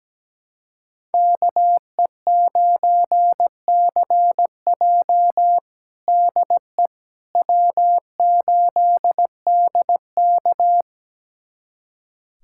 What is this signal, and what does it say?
Morse code, heard as a keyed tone, about 17 wpm, 700 Hz. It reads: KE9CJ DE W8DK
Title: USA Callsign DE Callsign 17wpm